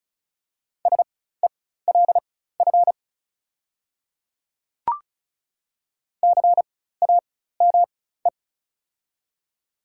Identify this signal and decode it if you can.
Morse code, heard as a keyed tone, about 36 wpm, spaced as for 16 wpm, 690 Hz. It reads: SELF CAME
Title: 4x Character Spacing - Top 500 Words - 35wpm